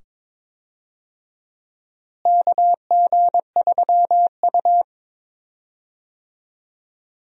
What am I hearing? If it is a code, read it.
Morse code, heard as a keyed tone, about 22 wpm, 700 Hz. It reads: KG3U
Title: USA Call Signs - 4 Characters - Repeated 2x - 22wpm